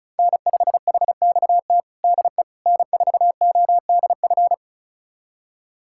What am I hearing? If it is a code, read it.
Morse code, heard as a keyed tone, about 36 wpm, 700 Hz. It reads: N5HXT DE N4ODF
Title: USA Callsign DE Callsign - 5 Characters - 35wpm